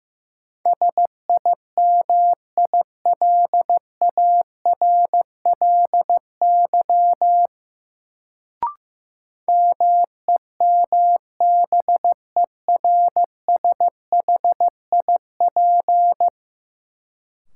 Morse code, heard as a keyed tone, about 15 wpm, 700 Hz. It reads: SIMILARLY MEMBERSHIP
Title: Sentences from Top 1400 Words - New Word Drill Alt - 15wpm